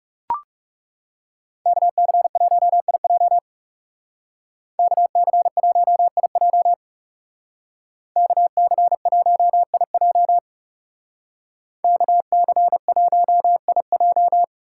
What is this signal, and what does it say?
Morse code, heard as a keyed tone, about 39 wpm, 700 Hz. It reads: KC1IJ KC1IJ KC1IJ KC1IJ